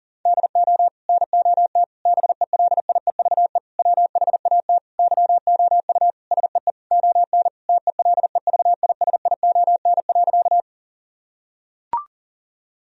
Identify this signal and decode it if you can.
Morse code, heard as a keyed tone, about 40 wpm, 700 Hz. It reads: DO NOT BELIEVE WHAT YOU SEE ON TELEVISION.
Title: Sentences from Top 1000 Words - Review Part 4 - 40wpm